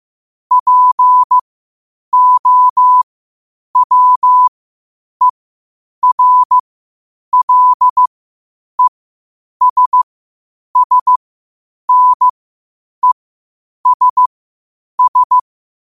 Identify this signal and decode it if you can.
Morse code, heard as a keyed tone, about 15 wpm, spaced as for 8 wpm, 1000 Hz. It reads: POWERLESSNESS
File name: qotdfiles / 8wpm